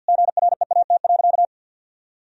Morse code, heard as a keyed tone, about 50 wpm, 700 Hz. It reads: GREAT.
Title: Sentences from Top 1000 Words - Review Part 2 - 50wpm